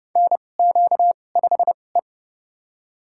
Morse code, heard as a keyed tone, about 29 wpm, spaced as for 21 wpm, 700 Hz. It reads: NQ5E
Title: USA Call Signs - 2x Character Spacing - 30wpm